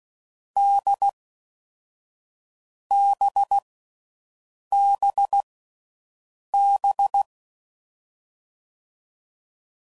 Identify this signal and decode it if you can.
Morse code, heard as a keyed tone, about 16 wpm, 800 Hz. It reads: D B B B
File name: Combinations / Difficult Combos / NDB NDB6